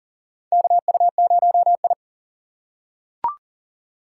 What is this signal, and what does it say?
Morse code, heard as a keyed tone, about 40 wpm, 700 Hz. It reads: KU0I
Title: USA Call Signs - 4 Characters - 4x Character Spacing - 40wpm